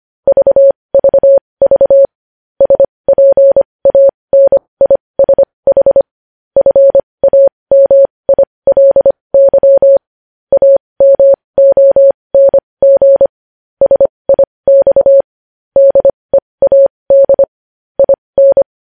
Morse code, heard as a keyed tone, about 25 wpm, spaced as for 20 wpm, 550 Hz. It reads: VVV SPANISH FAMILY AMONG SIX DEAD IN
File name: podcast / News headlines-1x-25x20